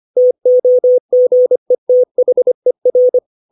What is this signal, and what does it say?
Morse code, heard as a keyed tone, about 25 wpm, 500 Hz. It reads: TOGETHER
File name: 632 words / 632 Words 25WPM